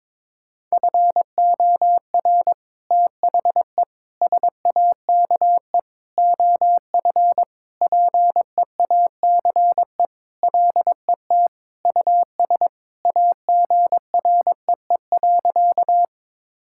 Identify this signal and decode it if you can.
Morse code, heard as a keyed tone, about 22 wpm, 700 Hz. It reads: FOR THE SAKE OF PEACE LET US AGREE.
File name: practice / 22wpm / Sentences with 4-Word Phrases - 22wpm